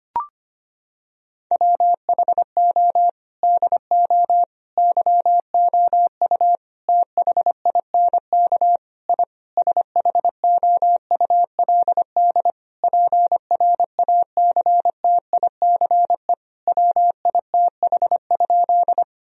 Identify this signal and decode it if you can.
Morse code, heard as a keyed tone, about 25 wpm, 700 Hz. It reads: WHO DO YOU THINK I SHOULD PRACTICE WITH?
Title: Sentences from Top 1000 Words - Review Part 4 - 25wpm